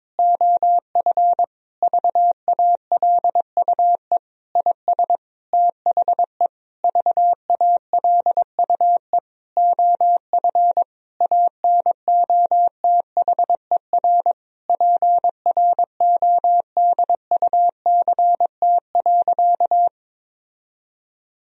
Morse code, heard as a keyed tone, about 22 wpm, 700 Hz. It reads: OF VALUE IS THE VALUE OF ANOTHER PRODUCT.
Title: Sentences from Top 1000 Words - Review Part 3 - 22wpm